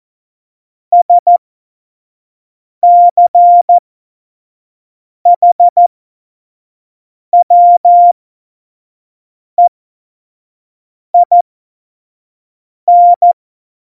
Morse code, heard as a keyed tone, about 14 wpm, spaced as for 5 wpm, 700 Hz. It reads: SCHWEIN
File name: Morsekurs / Witze / Witz 27. November 2018, 25 F70